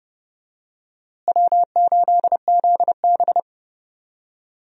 Morse code, heard as a keyed tone, about 30 wpm, 700 Hz. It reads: W8ZB
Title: USA Call Signs - 4 Characters - Repeated 3x - 30wpm